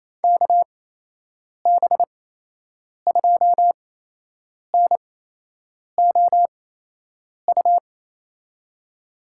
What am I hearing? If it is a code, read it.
Morse code, heard as a keyed tone, about 28 wpm, spaced as for 8 wpm, 700 Hz. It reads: KB2NOU
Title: USA Call Signs - 6 Characters - 8x Character Spacing - 28wpm